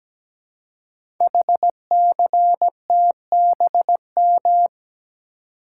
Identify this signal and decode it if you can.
Morse code, heard as a keyed tone, about 17 wpm, 700 Hz. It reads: HCTBM
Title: Code Groups - 5 Letters - 17wpm